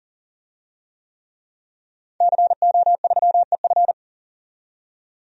Morse code, heard as a keyed tone, about 40 wpm, 700 Hz. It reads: CO3EF
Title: DXCC-100 Call Signs - 5 Characters - 40wpm